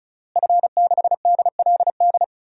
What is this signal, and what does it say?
Morse code, heard as a keyed tone, about 36 wpm, 700 Hz. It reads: F6DLD
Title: DXCC-100 Call Signs - 5 Characters - Repeated 3x - 35wpm